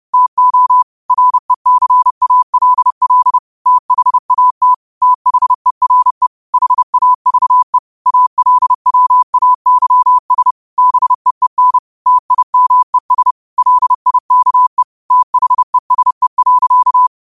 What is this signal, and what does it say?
Morse code, heard as a keyed tone, about 30 wpm, 1000 Hz. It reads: TO RECALL THAT THERE HAVE ALWAYS BEEN TIMES LIKE THESE.